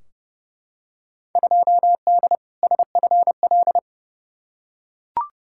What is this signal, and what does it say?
Morse code, heard as a keyed tone, about 32 wpm, 700 Hz. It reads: 2D SFL
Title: Field Day Exchanges Encore - 30wpm